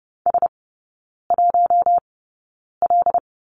Morse code, heard as a keyed tone, about 30 wpm, spaced as for 9 wpm, 700 Hz. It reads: S1L
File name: Lesson 18e - Letter m - Three Characters 30wpm